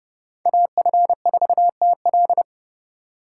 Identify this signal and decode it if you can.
Morse code, heard as a keyed tone, about 29 wpm, 700 Hz. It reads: AF4TL